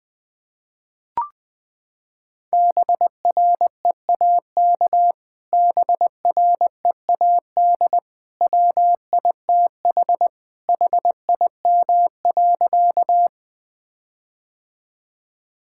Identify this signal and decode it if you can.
Morse code, heard as a keyed tone, about 20 wpm, 700 Hz. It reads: BREAK BREAD WITH HIM.
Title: Sentences from Top 1600 Words 20wpm